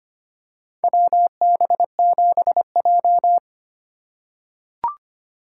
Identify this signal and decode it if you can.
Morse code, heard as a keyed tone, about 25 wpm, 700 Hz. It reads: WB7J
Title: USA Call Signs - 4 Characters - 8x Character Spacing - 25wpm